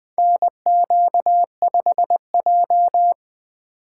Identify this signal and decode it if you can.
Morse code, heard as a keyed tone, about 20 wpm, 700 Hz. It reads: NQ5J
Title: USA Call Signs - 2x Character Spacing - 20wpm